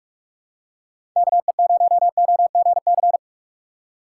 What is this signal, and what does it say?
Morse code, heard as a keyed tone, about 44 wpm, 700 Hz. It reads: KE0OGC